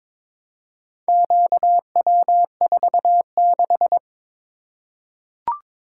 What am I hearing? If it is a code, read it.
Morse code, heard as a keyed tone, about 22 wpm, 700 Hz. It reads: QW46
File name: practice / 22wpm / Four Letter-Number Combinations - 22wpm